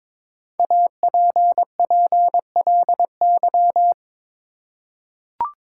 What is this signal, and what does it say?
Morse code, heard as a keyed tone, about 22 wpm, 690 Hz. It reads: APPLY
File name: 5 Letter Words - First 3 Letters Given - 22wpm